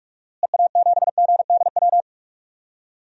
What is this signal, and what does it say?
Morse code, heard as a keyed tone, about 46 wpm, 700 Hz. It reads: EA7GDW
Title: DXCC-100 Call Signs - 6 Characters - Repeated 2x - 45wpm